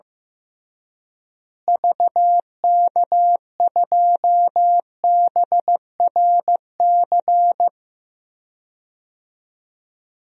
Morse code, heard as a keyed tone, about 15 wpm, 700 Hz. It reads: VK2BRC
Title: DXCC-100 Call Signs - 6 Characters - Repeated 3x - 15wpm